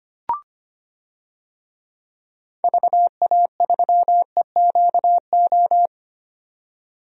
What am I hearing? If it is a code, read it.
Morse code, heard as a keyed tone, about 25 wpm, 700 Hz. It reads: VA3EQO
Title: DXCC-100 Call Signs - 6 Characters - 25wpm